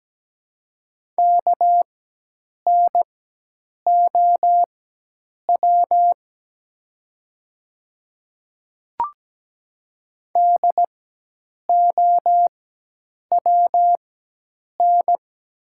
Morse code, heard as a keyed tone, about 17 wpm, spaced as for 8 wpm, 700 Hz. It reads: KNOW DOWN
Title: 4x Character Spacing - Top 200 Words - 17wpm